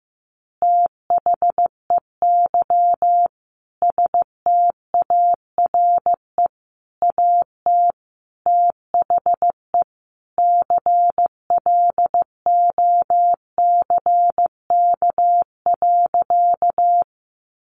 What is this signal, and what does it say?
Morse code, heard as a keyed tone, about 15 wpm, 700 Hz. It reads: THEY STARE AT THE CLOCK.
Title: Easy Phrases 15wpm